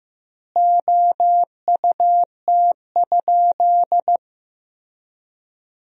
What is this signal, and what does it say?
Morse code, heard as a keyed tone, about 15 wpm, 700 Hz. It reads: OUT?
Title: 2x Word-Spacing - Sentences from Top 100 Words - 15wpm